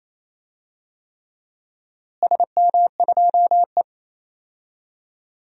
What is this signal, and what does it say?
Morse code, heard as a keyed tone, about 28 wpm, 700 Hz. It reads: SM2E
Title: DXCC-100 Call Signs - 4 Characters - Repeated 3x - 28wpm